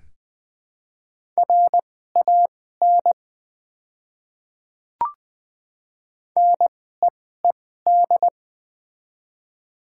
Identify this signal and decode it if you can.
Morse code, heard as a keyed tone, about 20 wpm, spaced as for 14 wpm, 690 Hz. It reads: RAN NEED
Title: 2x Character Spacing - Top 500 Words - 20wpm